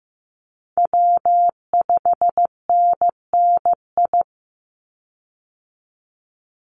Morse code, heard as a keyed tone, about 15 wpm, 700 Hz. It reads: W5NNI
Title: USA Call Signs 15wpm